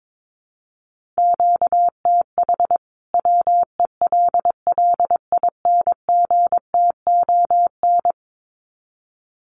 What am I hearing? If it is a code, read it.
Morse code, heard as a keyed tone, about 22 wpm, 700 Hz. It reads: QTH WELLINGTON
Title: Common QSO Phrases 22wpm